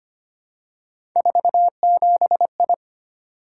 Morse code, heard as a keyed tone, about 25 wpm, 700 Hz. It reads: 47I
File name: Three Letter-Number Combinations - 25wpm